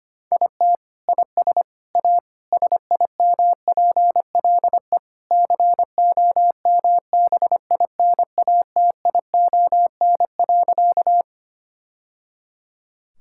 Morse code, heard as a keyed tone, about 25 wpm, 700 Hz. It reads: IT IS A SIMPLE COMBINATION.
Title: Sentences from Top 1000 Words - Review Part 6 - 25wpm